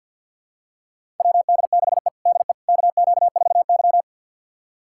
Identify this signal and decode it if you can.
Morse code, heard as a keyed tone, about 51 wpm, 700 Hz. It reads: WD6E DE KQ4Y
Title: USA Callsign DE Callsign - 4 Characters - 50wpm